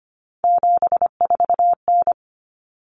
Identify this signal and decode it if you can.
Morse code, heard as a keyed tone, about 25 wpm, 700 Hz. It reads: 74N
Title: Common Cut Numbers - 3-Digit 25wpm